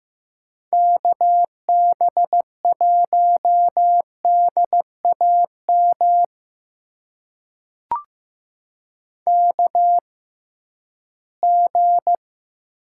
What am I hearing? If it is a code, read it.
Morse code, heard as a keyed tone, about 15 wpm, spaced as for 11 wpm, 700 Hz. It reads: KB1DAM KG
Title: USA Call Signs - 6 Characters - 6x Character Spacing - 15wpm